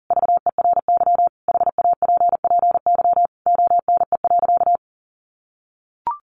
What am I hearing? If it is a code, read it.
Morse code, heard as a keyed tone, about 40 wpm, 700 Hz. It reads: VERY HAPPY ONE.
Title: Sentences from Top 1400 Words 40wpm